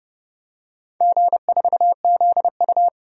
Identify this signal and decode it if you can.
Morse code, heard as a keyed tone, about 30 wpm, 700 Hz. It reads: G4ZU
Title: DXCC-100 Call Signs - 4 Characters - 30wpm